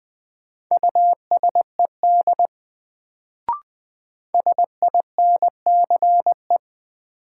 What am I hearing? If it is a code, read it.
Morse code, heard as a keyed tone, about 20 wpm, 700 Hz. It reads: USED SINCE
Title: Sets of 2 Words - New Word Drill - 20wpm